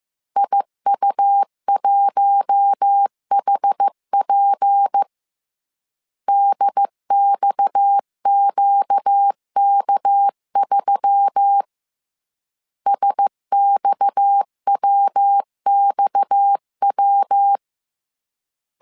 Morse code, heard as a keyed tone, about 15 wpm, 800 Hz. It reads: IU1HP DXQK3 SXWXW